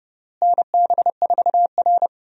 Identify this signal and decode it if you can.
Morse code, heard as a keyed tone, about 30 wpm, 700 Hz. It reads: NB4R
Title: USA Call Signs - 4 Characters - Repeated 2x - 30wpm